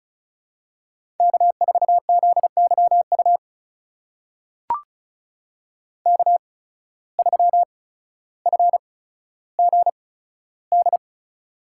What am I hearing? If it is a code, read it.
Morse code, heard as a keyed tone, about 36 wpm, spaced as for 15 wpm, 700 Hz. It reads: K4ZYU K3FGD
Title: USA Call Signs - 5 Characters - 8x Character Spacing - 35wpm